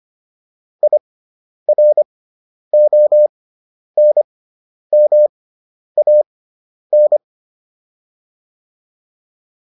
Morse code, heard as a keyed tone, about 25 wpm, spaced as for 10 wpm, 600 Hz. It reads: IRONMAN